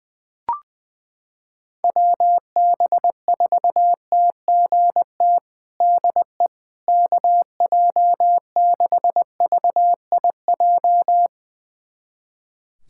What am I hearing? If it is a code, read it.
Morse code, heard as a keyed tone, about 21 wpm, 700 Hz. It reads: WB4TGT DE KJ6VIJ
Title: USA Callsign DE Callsign - 6 Characters - 20wpm